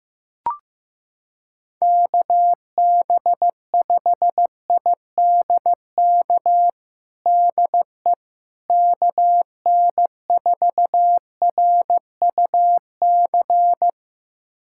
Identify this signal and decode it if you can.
Morse code, heard as a keyed tone, about 15 wpm, 700 Hz. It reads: KB5IDK DE KN4RUC
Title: USA Callsign DE Callsign - 6 Characters - 15wpm